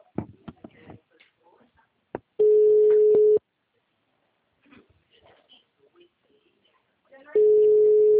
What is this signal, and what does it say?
Ringback tone (Europe, 425 Hz).
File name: out- -2007- - - .2